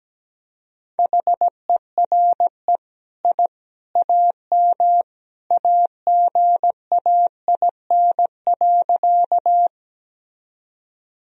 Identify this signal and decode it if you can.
Morse code, heard as a keyed tone, about 17 wpm, 700 Hz. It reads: HERE I AM AGAIN.
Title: Sentences from Top 1000 Words - Review Part 2 - 17wpm